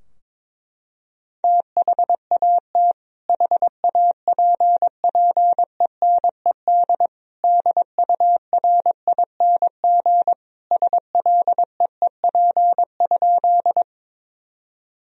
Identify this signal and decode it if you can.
Morse code, heard as a keyed tone, about 22 wpm, 700 Hz. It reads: THAT HAPPENED DURING SLEEP?
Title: Sentences from Top 1000 Words - Review Part 2 - 22wpm